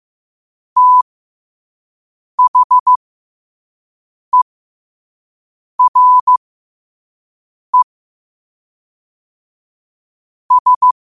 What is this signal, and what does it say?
Morse code, heard as a keyed tone, about 15 wpm, spaced as for 5 wpm, 1000 Hz. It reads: THERE S
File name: qotdfiles / 5wpm